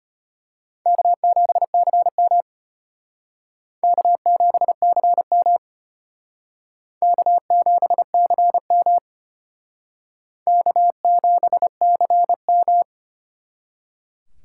Morse code, heard as a keyed tone, about 32 wpm, 700 Hz. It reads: K7CM K7CM K7CM K7CM